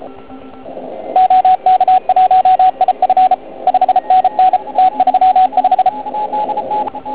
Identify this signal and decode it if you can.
Morse code, heard as a keyed tone, about 33 wpm, 750 Hz, with radio noise. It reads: OK1IF 5NN T35
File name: Archiv / CUC129 / CUC129 pileup